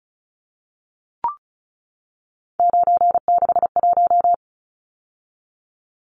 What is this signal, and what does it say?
Morse code, heard as a keyed tone, about 34 wpm, 700 Hz. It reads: 961
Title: Numbers - 3-Digit 35wpm